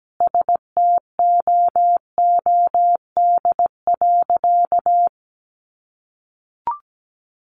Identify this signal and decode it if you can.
Morse code, heard as a keyed tone, about 17 wpm, 700 Hz. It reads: STOOD.
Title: Sentences from Top 500 Words 17wpm